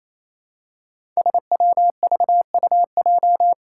Morse code, heard as a keyed tone, about 27 wpm, 700 Hz. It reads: SWVUJ
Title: Five Challenging Letter-Number Combinations 28wpm